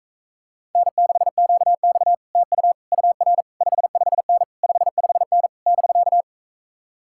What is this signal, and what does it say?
Morse code, heard as a keyed tone, about 45 wpm, 700 Hz. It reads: N6QX TU UR 55N 55N <BK>